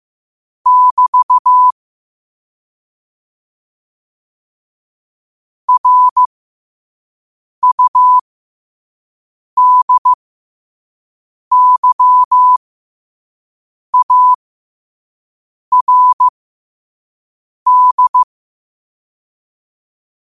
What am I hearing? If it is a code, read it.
Morse code, heard as a keyed tone, about 15 wpm, spaced as for 5 wpm, 1000 Hz. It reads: <BT> RUDYARD
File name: qotdfiles / 5wpm